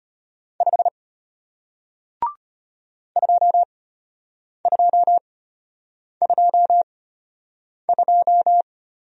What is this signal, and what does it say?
Morse code, heard as a keyed tone, about 34 wpm, spaced as for 8 wpm, 700 Hz. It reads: 5 2222